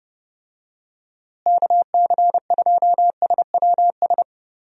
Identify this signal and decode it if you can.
Morse code, heard as a keyed tone, about 30 wpm, 700 Hz. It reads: KC2SWS
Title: USA Call Signs - 6 Characters - Repeated 3x - 30wpm